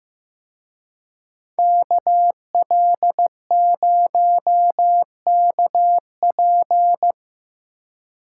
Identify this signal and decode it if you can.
Morse code, heard as a keyed tone, about 17 wpm, 700 Hz. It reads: KL0KP
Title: Speed Racing - USA 2x2 Call Signs 15wpm